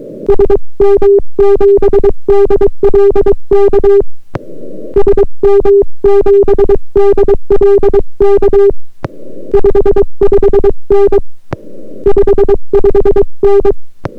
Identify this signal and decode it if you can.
Morse code, heard as a keyed tone, about 22 wpm, 400 Hz. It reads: SM7DLK SM7DLK 55N 55N